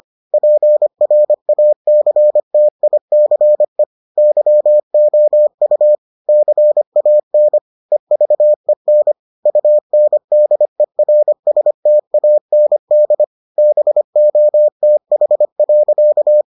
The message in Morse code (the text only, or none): PRACTICE YOU CAN EVEN UNDERSTAND BOTH.